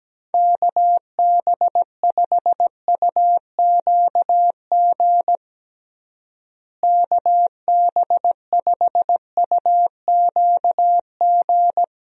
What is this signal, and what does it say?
Morse code, heard as a keyed tone, about 17 wpm, 700 Hz. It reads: KB5UQG KB5UQG
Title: USA Call Signs - 6 Characters - Repeated 2x - 17wpm